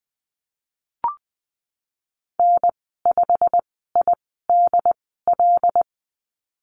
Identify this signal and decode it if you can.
Morse code, heard as a keyed tone, about 20 wpm, spaced as for 15 wpm, 700 Hz. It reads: N5IDL
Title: 173 Lesson 28g - Number 9 - Call Signs 20wpm